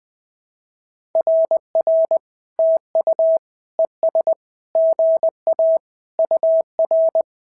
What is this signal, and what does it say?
Morse code, heard as a keyed tone, about 20 wpm, 650 Hz. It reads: RR TU ES GA UR
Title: SOTA QSO Practice 20wpm